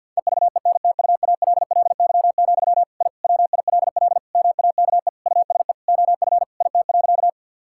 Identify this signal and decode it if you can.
Morse code, heard as a keyed tone, about 50 wpm, 700 Hz. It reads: EVENTUALLY, I WILL MAKE USE OF IT.
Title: Sentences from Top 1000 Words - Review Part 6 - 50wpm